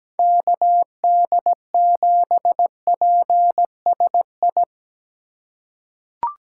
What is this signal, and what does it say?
Morse code, heard as a keyed tone, about 17 wpm, 700 Hz. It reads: KD7PSI
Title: USA Call Signs - 6 Characters - 2x Character Spacing - 17wpm